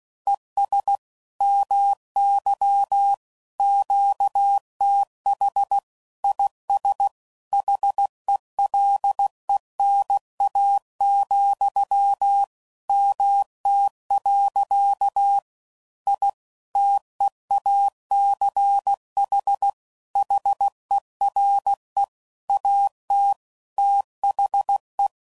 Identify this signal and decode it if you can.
Morse code, heard as a keyed tone, about 16 wpm, 800 Hz. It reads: ES MY QTH IS HELENA, MT. I TEACH HERE AT THE
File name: CW Code Groups @15WPM / QSOs / WD0BGP Spaced